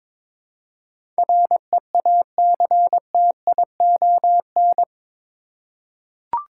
Morse code, heard as a keyed tone, about 22 wpm, 700 Hz. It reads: REACTION
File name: practice / 22wpm / Eight Letter Words - 22wpm